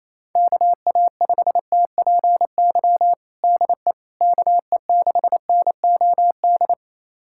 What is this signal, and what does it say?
Morse code, heard as a keyed tone, about 27 wpm, 700 Hz. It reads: KA5TPY DE KE6NOD
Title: USA Callsign DE Callsign - 6 Characters - 28wpm